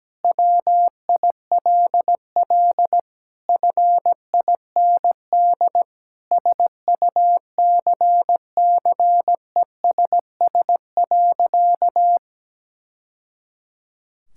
Morse code, heard as a keyed tone, about 17 wpm, 700 Hz. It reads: WILL FIND SUCCESS.
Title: Sentences from Top 1000 Words - Review Part 5 - 17wpm